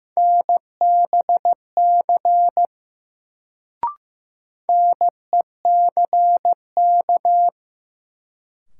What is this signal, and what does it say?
Morse code, heard as a keyed tone, about 15 wpm, 700 Hz. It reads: NBC NECK
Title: Sets of 5 Words Encore - New Word Drill - 15wpm